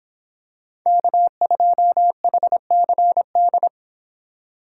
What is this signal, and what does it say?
Morse code, heard as a keyed tone, about 26 wpm, 700 Hz. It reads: K2HCD